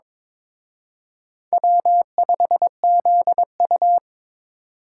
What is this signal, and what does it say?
Morse code, heard as a keyed tone, about 26 wpm, 700 Hz. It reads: W5ZU
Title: Speed Racing - USA Call Signs 22wpm